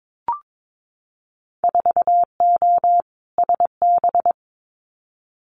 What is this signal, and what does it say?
Morse code, heard as a keyed tone, about 22 wpm, 700 Hz. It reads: 4O SB
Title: Winter Field Day Exchanges 22wpm